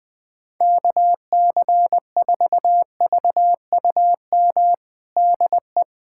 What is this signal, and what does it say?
Morse code, heard as a keyed tone, about 20 wpm, 700 Hz. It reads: KC4VUM DE
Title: USA Callsign DE Callsign - 6 Characters - 20wpm